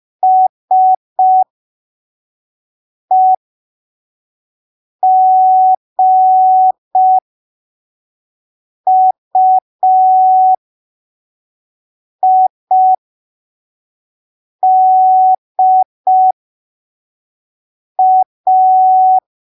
Morse code, heard as a keyed tone, about 5 wpm, spaced as for 3 wpm, 750 Hz. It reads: SEGUIDA
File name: arquivos / texto / texto 08 05ppm